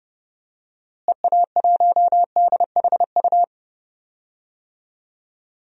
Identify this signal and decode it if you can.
Morse code, heard as a keyed tone, about 30 wpm, 700 Hz. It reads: EA1DHU
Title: DXCC-100 Call Signs - 6 Characters - Repeated 3x - 30wpm